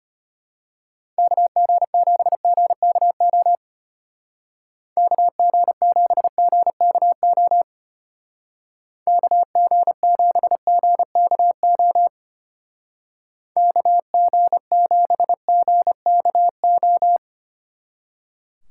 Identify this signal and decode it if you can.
Morse code, heard as a keyed tone, about 32 wpm, 700 Hz. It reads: KG7GKO KG7GKO KG7GKO KG7GKO